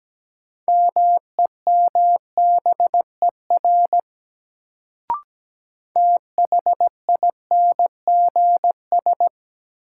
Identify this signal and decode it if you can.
Morse code, heard as a keyed tone, about 17 wpm, 700 Hz. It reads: MEMBER THINGS
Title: Sets of 3 Words - New Word Drill - 17wpm